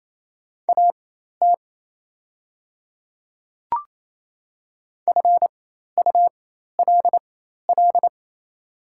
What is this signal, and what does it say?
Morse code, heard as a keyed tone, about 28 wpm, spaced as for 13 wpm, 700 Hz. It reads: AT FULL
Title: 4x Character Spacing - Top 500 Words - 28wpm